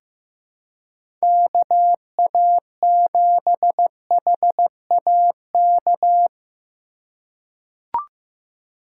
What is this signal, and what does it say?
Morse code, heard as a keyed tone, about 15 wpm, 700 Hz. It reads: KA7HAK
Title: USA Call Signs - 6 Characters - 2x Character Spacing - 15wpm